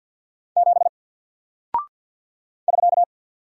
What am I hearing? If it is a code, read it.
Morse code, heard as a keyed tone, about 48 wpm, 700 Hz. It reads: 7 <SK>
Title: Single Character (Letter:Number:Punctuation:Prosign) Alt 50wpm